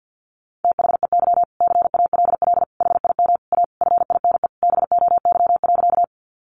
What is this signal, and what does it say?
Morse code, heard as a keyed tone, about 50 wpm, 700 Hz. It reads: THEY CALL HIM A FINE BOY.
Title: Sentences from Top 500 Words 50wpm